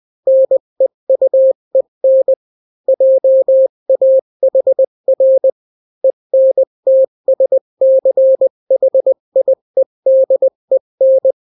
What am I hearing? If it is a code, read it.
Morse code, heard as a keyed tone, about 20 wpm, 530 Hz. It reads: NEUEN JAHR ENTSCHIEDEN
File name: daily / 18wpm